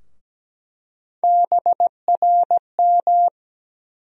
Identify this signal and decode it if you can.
Morse code, heard as a keyed tone, about 17 wpm, 700 Hz. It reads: BRM
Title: USA Call Sign Postfixes - 3 Letters 17wpm